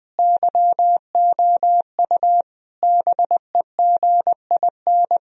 Morse code, heard as a keyed tone, about 20 wpm, 700 Hz. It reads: YOU BEGIN
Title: Sentences from Top 1000 Words - Review Part 1 - 20wpm